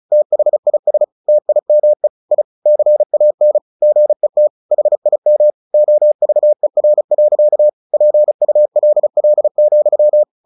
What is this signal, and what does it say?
Morse code, heard as a keyed tone, about 35 wpm, 600 Hz. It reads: THIS TIME I CAN GET HIM OVER. PULL,
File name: the old man and the sea 2 35wpm